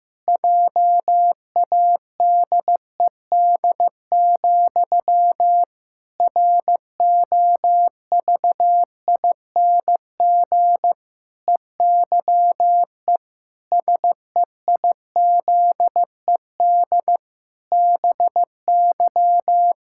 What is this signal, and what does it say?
Morse code, heard as a keyed tone, about 15 wpm, 700 Hz. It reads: JADED, ROVING EYE SEIZED BY
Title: Pangram Challenge v2 - 15wpm